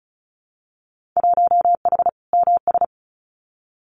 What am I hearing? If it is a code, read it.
Morse code, heard as a keyed tone, about 34 wpm, 700 Hz. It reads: 1H MS
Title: Winter Field Day Exchanges 35wpm